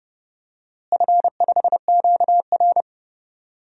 Morse code, heard as a keyed tone, about 30 wpm, 700 Hz. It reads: F5QR
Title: DXCC-100 Call Signs - 4 Characters - Repeated 2x - 30wpm